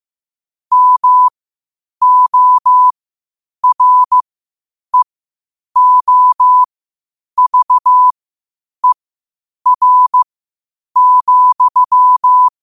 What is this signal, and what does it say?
Morse code, heard as a keyed tone, about 15 wpm, spaced as for 8 wpm, 1000 Hz. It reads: MOREOVER,
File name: qotdfiles / 8wpm